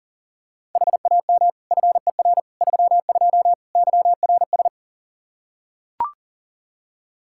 Morse code, heard as a keyed tone, about 40 wpm, 700 Hz. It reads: HAM FER 32 YRS
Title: K1USN SST QSO Practice 40wpm